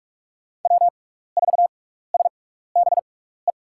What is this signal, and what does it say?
Morse code, heard as a keyed tone, about 44 wpm, spaced as for 15 wpm, 700 Hz. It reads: W4SBE
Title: USA Call Signs - 6x Character Spacing - 45wpm